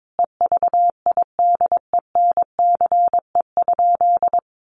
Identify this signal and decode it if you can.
Morse code, heard as a keyed tone, about 22 wpm, 700 Hz. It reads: EVIDENCE?
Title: Sentences from Top 1300 Words 22wpm